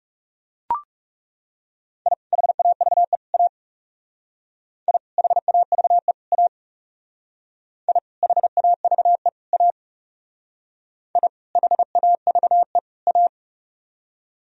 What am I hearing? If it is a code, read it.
Morse code, heard as a keyed tone, about 37 wpm, 700 Hz. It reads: I HAVE A I HAVE A I HAVE A I HAVE A
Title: Speed-Racing - Sets of 3 Words 30wpm